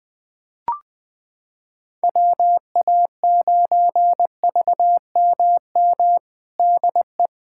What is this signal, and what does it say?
Morse code, heard as a keyed tone, about 21 wpm, 700 Hz. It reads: WA9VMM DE
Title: USA Callsign DE Callsign - 6 Characters - 20wpm